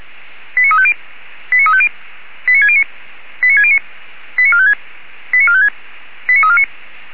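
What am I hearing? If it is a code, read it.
Five-tone selective call (ZVEI), heard as quick runs of five tones, tones of about 70 ms. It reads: 89350 89350 89609 89609 89456 89456 89349
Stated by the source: ZVEI five-tone selective call 89347, received on an SDR.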